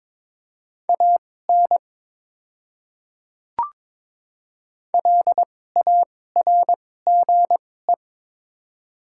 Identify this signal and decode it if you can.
Morse code, heard as a keyed tone, about 21 wpm, spaced as for 16 wpm, 700 Hz. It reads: AN LARGE
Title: 2x Character Spacing - Top 200 Words - 22wpm